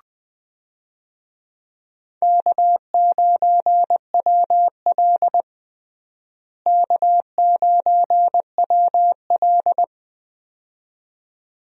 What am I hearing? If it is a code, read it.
Morse code, heard as a keyed tone, about 20 wpm, 700 Hz. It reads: K9WL K9WL
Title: USA Call Signs - 4 Characters - Repeated 2x - 20wpm